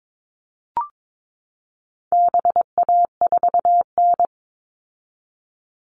Morse code, heard as a keyed tone, about 22 wpm, 700 Hz. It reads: BA4N
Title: All Cut Numbers - 4-Digit 22wpm